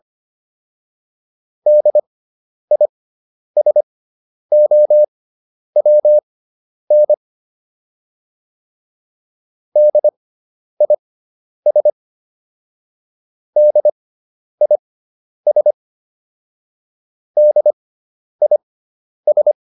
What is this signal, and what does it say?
Morse code, heard as a keyed tone, about 25 wpm, spaced as for 10 wpm, 600 Hz. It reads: DISOWN DIS DIS DIS